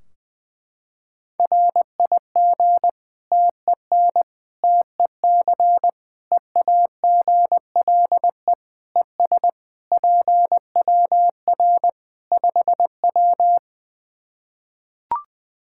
Morse code, heard as a keyed tone, about 20 wpm, 700 Hz. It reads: RIG TEN TEC EAGLE ES PWR 5W
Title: Practice QSO - Exchange 2 - Info - 20wpm